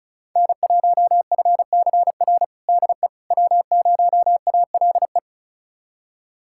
Morse code, heard as a keyed tone, about 34 wpm, 690 Hz. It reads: N1FCR DE W0ALE
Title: USA Callsign DE Callsign - 5 Characters - 35wpm